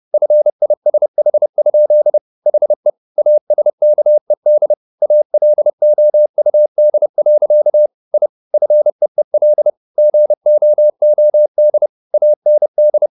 Morse code, heard as a keyed tone, about 30 wpm, 600 Hz. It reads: FISH? HE ASKED ALOUD. I FEEL GOOD AND